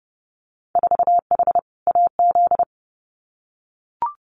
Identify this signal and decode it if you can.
Morse code, heard as a keyed tone, about 30 wpm, 700 Hz. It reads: 4H AZ
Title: Winter Field Day Exchanges Encore 30wpm